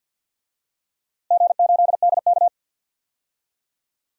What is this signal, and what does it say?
Morse code, heard as a keyed tone, about 47 wpm, 700 Hz. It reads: G8DK
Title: DXCC-100 Call Signs - 4 Characters - Repeated 3x - 50wpm